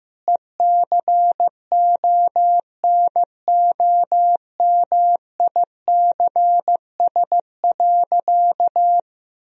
Morse code, heard as a keyed tone, about 15 wpm, 700 Hz. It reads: ECONOMICS.
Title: Sentences from Top 1000 Words - Review Part 3 - 15wpm